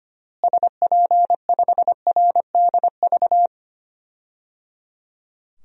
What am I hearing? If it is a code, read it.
Morse code, heard as a keyed tone, about 25 wpm, 700 Hz. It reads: SP5RDV
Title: DXCC-100 Call Signs - 6 Characters - Repeated 3x - 25wpm